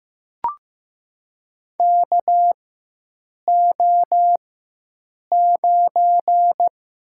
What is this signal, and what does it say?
Morse code, heard as a keyed tone, about 15 wpm, 700 Hz. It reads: K O 9
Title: USA Call Signs - 4 Characters - 4x Character Spacing - 15wpm